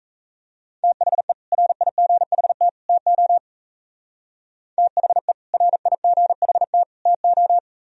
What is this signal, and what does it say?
Morse code, heard as a keyed tone, about 33 wpm, 700 Hz. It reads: THE RIGHT TO THE RIGHT TO
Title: Speed-Racing - Sets of 3 Words 28wpm